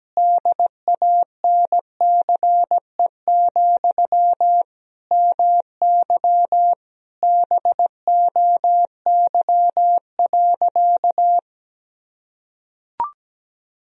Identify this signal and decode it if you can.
Morse code, heard as a keyed tone, about 17 wpm, 700 Hz. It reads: DANCE, MY BOY.
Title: Sentences from Top 1000 Words - Review Part 4 - 17wpm